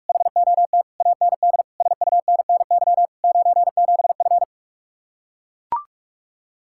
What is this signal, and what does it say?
Morse code, heard as a keyed tone, about 44 wpm, 700 Hz. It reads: HOT AND SUNNY 97F